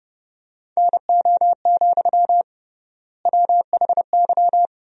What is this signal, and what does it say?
Morse code, heard as a keyed tone, about 30 wpm, 700 Hz. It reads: NO, WHY
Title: 3x Word-Spacing - Sentences from Top 300 Words - 30wpm